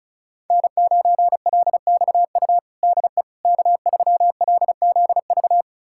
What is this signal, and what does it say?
Morse code, heard as a keyed tone, about 34 wpm, 690 Hz. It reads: N9LXU DE K3LZV
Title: USA Callsign DE Callsign - 5 Characters - 35wpm